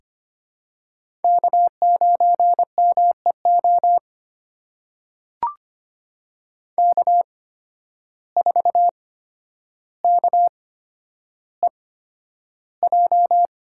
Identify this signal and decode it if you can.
Morse code, heard as a keyed tone, about 25 wpm, spaced as for 11 wpm, 700 Hz. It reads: K9MEO K4KEJ
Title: USA Call Signs - 5 Characters - 8x Character Spacing - 25wpm